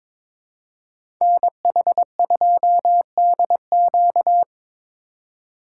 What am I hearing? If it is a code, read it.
Morse code, heard as a keyed tone, about 24 wpm, 700 Hz. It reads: NH2DQ
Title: Speed Racing - USA Call Signs 22wpm